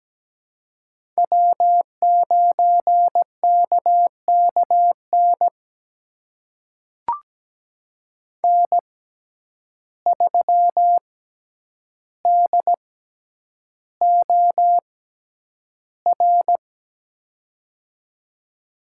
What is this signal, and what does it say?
Morse code, heard as a keyed tone, about 17 wpm, spaced as for 9 wpm, 700 Hz. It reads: W9KKN N3DOR
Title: USA Call Signs - 5 Characters - 6x Character Spacing - 17wpm